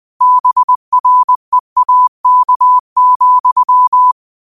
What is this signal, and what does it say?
Morse code, heard as a keyed tone, about 20 wpm, 1000 Hz. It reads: BREAK,
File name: qotdfiles / 20wpm